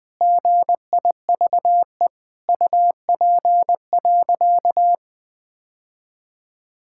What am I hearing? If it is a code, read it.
Morse code, heard as a keyed tone, about 20 wpm, 700 Hz. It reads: GIVE UP.
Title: Sentences with 3-Word Phrases - 20wpm